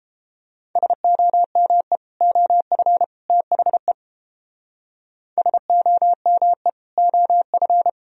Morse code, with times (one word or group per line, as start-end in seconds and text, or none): 0.8-2.0 SOME
2.2-3.0 OF
3.3-3.9 THE
5.4-6.7 SOME
7.0-7.9 OF